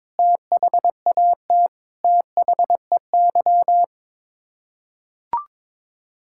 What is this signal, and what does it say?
Morse code, heard as a keyed tone, about 21 wpm, 700 Hz. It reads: THAT THEY
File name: Sets of 9 Words - 22wpm